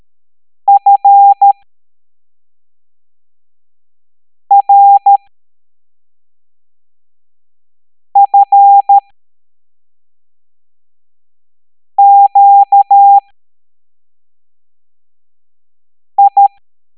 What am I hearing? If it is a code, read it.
Morse code, heard as a keyed tone, about 13 wpm, spaced as for 3 wpm, 800 Hz. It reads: FRFQI
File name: hamdata / L07-EX03-HQP